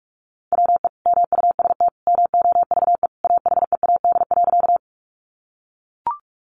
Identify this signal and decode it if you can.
Morse code, heard as a keyed tone, about 44 wpm, 700 Hz. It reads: WE MUST MOVE AHEAD.
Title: Sentences from Top 1100 Words 45wpm